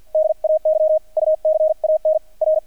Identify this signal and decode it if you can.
Morse code, heard as a keyed tone, about 48 wpm, 620 Hz. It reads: DAY U CAN R
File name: archived / CRSnet